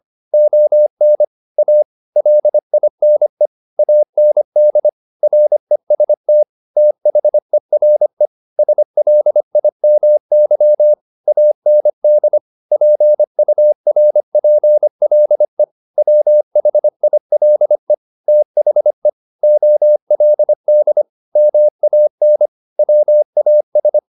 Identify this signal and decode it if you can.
Morse code, heard as a keyed tone, about 25 wpm, 600 Hz. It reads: ON A LINE AND REST THERE SLIMY AND PURPLE WHILE THE OLD MAN WAS
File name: the old man and the sea part 1 25wpm